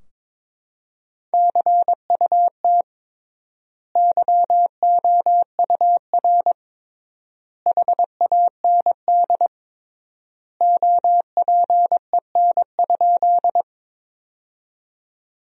Morse code, heard as a keyed tone, about 22 wpm, 700 Hz. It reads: CUT YOUR HAND OPEN?
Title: 3x Word-Spacing - Sentences from Top 300 Words - 22wpm